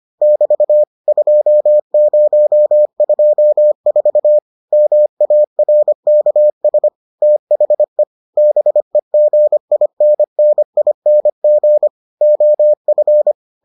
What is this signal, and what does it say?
Morse code, heard as a keyed tone, about 25 wpm, 600 Hz. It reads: <BT> 2024 MARKS THE BEGINNING OF